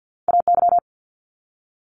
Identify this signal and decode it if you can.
Morse code, heard as a keyed tone, about 50 wpm, 700 Hz. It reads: AY
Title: Multi-letter Phonemes 50wpm